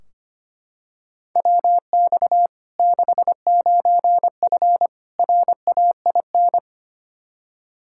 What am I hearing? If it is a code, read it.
Morse code, heard as a keyed tone, about 25 wpm, 700 Hz. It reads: WX 69F RAIN
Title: Practice QSO - Exchange 2 - Info - 25wpm